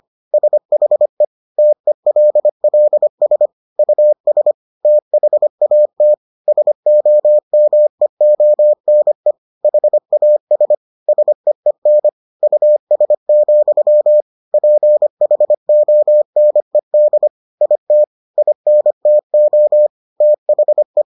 Morse code, heard as a keyed tone, about 25 wpm, 600 Hz. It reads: SHE TELLS US THAT SOMEONE HAS SEEN US, PHONED IT INTO THE